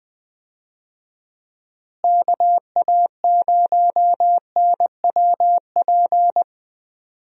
Morse code, heard as a keyed tone, about 20 wpm, 700 Hz. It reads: KA0NWP
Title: USA Call Signs - 6 Characters - Repeated 2x - 20wpm